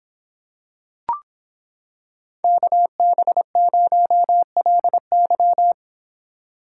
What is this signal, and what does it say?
Morse code, heard as a keyed tone, about 26 wpm, 700 Hz. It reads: KB0LY